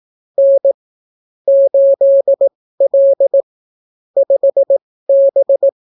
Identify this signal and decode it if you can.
Morse code, heard as a keyed tone, about 18 wpm, spaced as for 15 wpm, 550 Hz. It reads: N 8L 5B